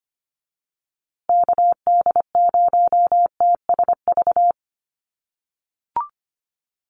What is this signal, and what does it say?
Morse code, heard as a keyed tone, about 25 wpm, 700 Hz. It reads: KD0TSV
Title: USA Call Signs Repeated 2x 25wpm